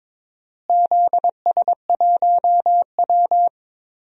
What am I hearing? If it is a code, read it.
Morse code, heard as a keyed tone, about 22 wpm, 700 Hz. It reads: ZS1W
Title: DXCC-100 Call Signs - 4 Characters - Repeated 2x - 22wpm